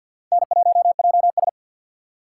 Morse code, heard as a keyed tone, about 47 wpm, 700 Hz. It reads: N1JS
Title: USA Call Signs - 4 Characters - 2x Character Spacing - 50wpm